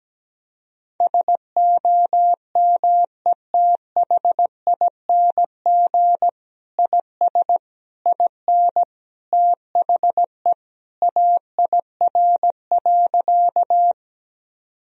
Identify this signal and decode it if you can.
Morse code, heard as a keyed tone, about 17 wpm, 700 Hz. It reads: SOMETHING IS IN THE AIR.
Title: Sentences from Top 1000 Words - Review Part 1 - 17wpm